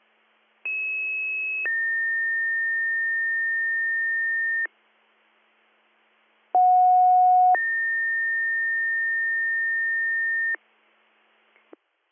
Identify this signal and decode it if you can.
Two-tone page, heard as a one-second tone, then a three-second tone, about 2600 then 1800 Hz, about 730 then 1800 Hz. It is heard from 0.7 to 10.6 s.